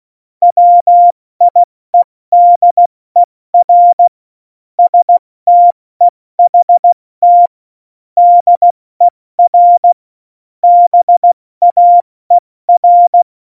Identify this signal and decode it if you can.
Morse code, heard as a keyed tone, about 16 wpm, 700 Hz. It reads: WIEDER STEHT DER BAER
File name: Morsekurs